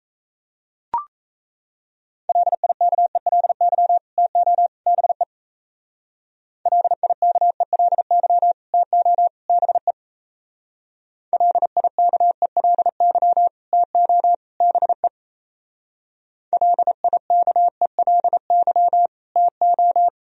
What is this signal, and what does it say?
Morse code, heard as a keyed tone, about 34 wpm, 700 Hz. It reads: LIKELY TO BE LIKELY TO BE LIKELY TO BE LIKELY TO
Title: Speed-Racing - Sets of 3 Words 28wpm